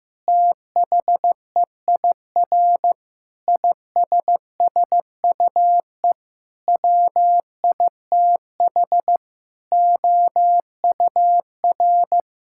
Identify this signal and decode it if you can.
Morse code, heard as a keyed tone, about 14 wpm, 700 Hz. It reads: THEIR ISSUE WITH OUR
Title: Sentences from Top 1000 Words - Review Part 6 - 15wpm